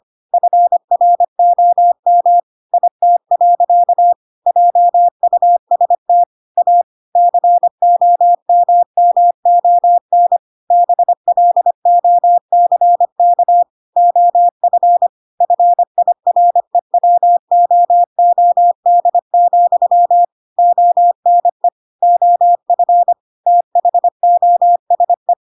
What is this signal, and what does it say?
Morse code, heard as a keyed tone, about 25 wpm, 700 Hz. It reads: FROM IT. JUST A COMMON BLOCK OF FIREWOOD, ONE OF THOSE
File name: Pinocchio / Pinocchio ch1 25 25 ik1hge